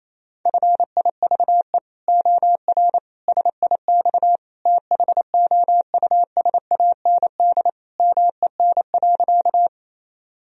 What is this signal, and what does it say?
Morse code, heard as a keyed tone, about 27 wpm, 700 Hz. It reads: FIVE OR SIX THOUSAND MEN.
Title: Sentences from Top 1000 Words - Review Part 3 - 28wpm